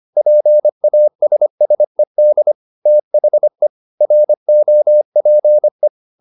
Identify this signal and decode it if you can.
Morse code, heard as a keyed tone, about 25 wpm, 600 Hz. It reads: PASSED THE ROPE